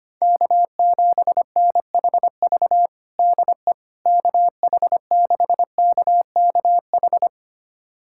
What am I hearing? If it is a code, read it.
Morse code, heard as a keyed tone, about 25 wpm, 700 Hz. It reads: K7NHV DE KH6KKH
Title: USA Callsign DE Callsign 25wpm